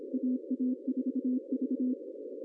Morse code, heard as a keyed tone, about 26 wpm, 260 Hz. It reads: AA4V
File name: CW / cqww-cw-k5tr